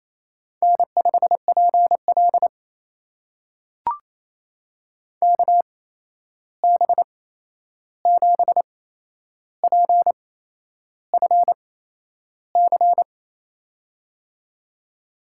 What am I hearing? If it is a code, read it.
Morse code, heard as a keyed tone, about 28 wpm, spaced as for 11 wpm, 700 Hz. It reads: N5PL KB7PFC
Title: USA Call Signs - 8x Character Spacing - 28wpm